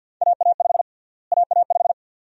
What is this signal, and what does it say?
Morse code, heard as a keyed tone, about 53 wpm, 700 Hz. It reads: AA5 AA5
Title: USA Call Sign Prefixes - Repeated 2x - 50wpm